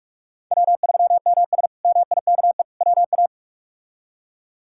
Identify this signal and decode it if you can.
Morse code, heard as a keyed tone, about 42 wpm, 700 Hz. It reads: W3MS MIKE WA
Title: K1USN SST QSO Practice - 45wpm